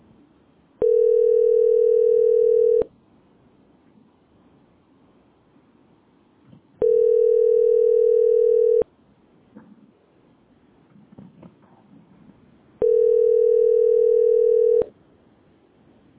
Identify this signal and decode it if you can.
Ringback tone (North America), 440 + 480 Hz.